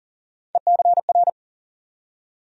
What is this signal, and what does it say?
Morse code, heard as a keyed tone, about 40 wpm, 700 Hz. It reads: ECR
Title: USA Call Sign Postfixes - 3 Letters 40wpm